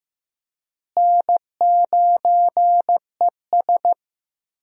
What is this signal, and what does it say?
Morse code, heard as a keyed tone, about 15 wpm, 700 Hz. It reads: N9ES
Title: USA Call Signs - 4 Characters - Repeated 3x - 15wpm